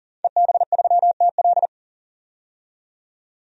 Morse code, heard as a keyed tone, about 40 wpm, 700 Hz. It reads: EB3TL
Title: DXCC-100 Call Signs - 5 Characters - 40wpm